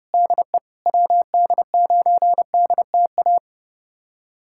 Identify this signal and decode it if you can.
Morse code, heard as a keyed tone, about 30 wpm, 700 Hz. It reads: DE WD9DTA
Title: USA Callsign DE Callsign - 6 Characters - 30wpm